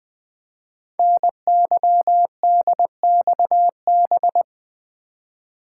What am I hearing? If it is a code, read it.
Morse code, heard as a keyed tone, about 20 wpm, 700 Hz. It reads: NYDXB
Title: Code Groups - 5 Letters - 20wpm